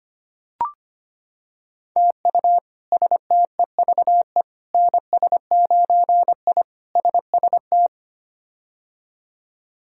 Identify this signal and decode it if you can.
Morse code, heard as a keyed tone, about 25 wpm, 700 Hz. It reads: TU STEVE NS9I SST
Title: K1USN SST QSO Practice - 25wpm